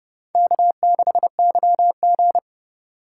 Morse code, heard as a keyed tone, about 30 wpm, 700 Hz. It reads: K6YG